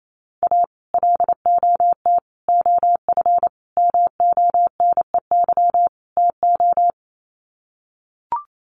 Morse code, heard as a keyed tone, about 28 wpm, 700 Hz. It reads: A LOT OF MONEY TO
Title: Sets of 5 Words Encore2 28wpm